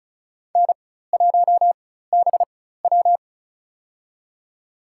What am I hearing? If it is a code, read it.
Morse code, heard as a keyed tone, about 34 wpm, spaced as for 16 wpm, 700 Hz. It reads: N1BW
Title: USA Call Signs - 4 Characters - 4x Character Spacing - 35wpm